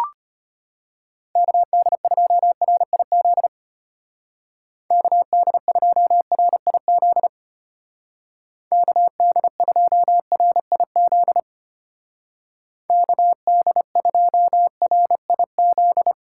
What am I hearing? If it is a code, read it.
Morse code, heard as a keyed tone, about 30 wpm, 700 Hz. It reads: KD2RIZ KD2RIZ KD2RIZ KD2RIZ
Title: Speed Racing - USA 2x3 Call Signs 25wpm